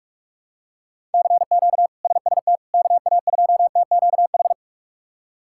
Morse code, heard as a keyed tone, about 46 wpm, 700 Hz. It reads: CQ SST KA2TQH
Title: K1USN SST QSO Practice - 45wpm